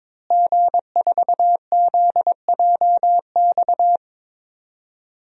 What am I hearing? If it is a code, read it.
Morse code, heard as a keyed tone, about 22 wpm, 700 Hz. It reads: G4ZJX
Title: DXCC-100 Call Signs - 5 Characters - Repeated 3x - 22wpm